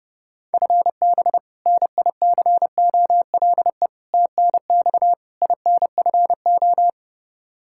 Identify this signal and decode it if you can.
Morse code, heard as a keyed tone, about 29 wpm, 700 Hz. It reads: FB NICOLE TNX INFO
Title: Practice QSO - Exchange 3 - More Info - 30wpm